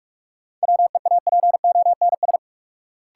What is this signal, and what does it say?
Morse code, heard as a keyed tone, about 45 wpm, 700 Hz. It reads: WEAPONS
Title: Sets of 5 Words Encore - New Word Drill - 45wpm